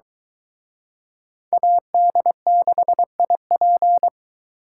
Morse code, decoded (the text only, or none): AD6IP